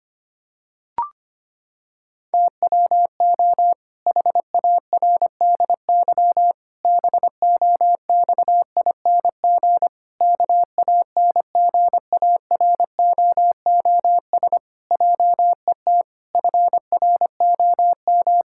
Morse code, heard as a keyed tone, about 25 wpm, 700 Hz. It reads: TWO HARDY BOXING KANGAROOS JET FROM
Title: Pangram Challenge v2 - 25wpm